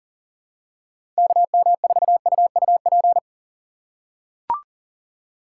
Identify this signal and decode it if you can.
Morse code, heard as a keyed tone, about 40 wpm, 700 Hz. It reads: KM4UUP
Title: USA Call Signs - 2x Character Spacing - 40wpm